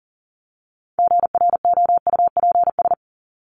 Morse code, heard as a keyed tone, about 40 wpm, 700 Hz. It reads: GROUPS
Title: ES:S Suffix - Top 100 Words with Base Words 40wpm